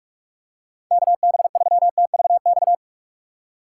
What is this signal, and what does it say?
Morse code, heard as a keyed tone, about 42 wpm, 700 Hz. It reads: KB3TVX
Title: USA Call Signs - 6 Characters - 6x Character Spacing - 45wpm